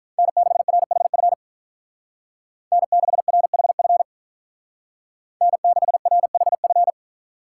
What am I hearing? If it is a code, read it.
Morse code, heard as a keyed tone, about 43 wpm, 700 Hz. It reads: N6RHF N6RHF N6RHF
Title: Speed Racing - USA 1x3 Call Signs 35wpm